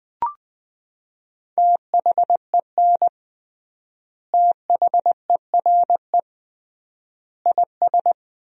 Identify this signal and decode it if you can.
Morse code, heard as a keyed tone, about 21 wpm, 700 Hz. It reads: THEN THERE IS
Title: 3x Word-Spacing - Sentences from Top 100 Words - 20wpm